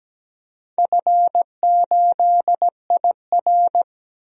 Morse code, heard as a keyed tone, about 17 wpm, 700 Hz. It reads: F8IR
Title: DXCC-100 Call Signs - 4 Characters - 17wpm